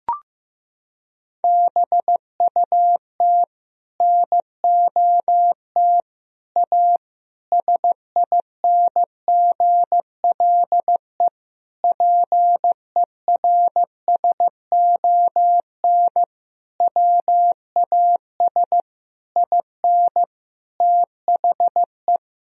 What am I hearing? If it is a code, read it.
Morse code, heard as a keyed tone, about 15 wpm, 700 Hz. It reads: BUT NOT A SINGLE PERSON WAS IN THE
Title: Sentences from Top 1000 Words - Review Part 4 - 15wpm